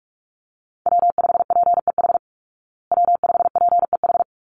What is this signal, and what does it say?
Morse code, heard as a keyed tone, about 44 wpm, 700 Hz. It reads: W5PEH W5PEH
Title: Call Signs Repeated 3x - 45wpm